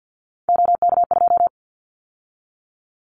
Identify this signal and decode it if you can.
Morse code, heard as a keyed tone, about 50 wpm, 700 Hz. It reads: OK2
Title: Callsign Prefixes for Top 100 DXCC - 50wpm